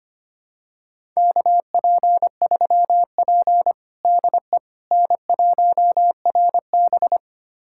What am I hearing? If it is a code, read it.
Morse code, heard as a keyed tone, about 25 wpm, 700 Hz. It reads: KP3P DE N1RB
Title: USA Callsign DE Callsign - 4 Characters - 25wpm